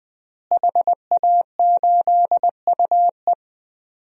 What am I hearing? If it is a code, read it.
Morse code, heard as a keyed tone, about 20 wpm, 700 Hz. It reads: HA8UE
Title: DXCC-100 Call Signs - 5 Characters - Repeated 2x - 20wpm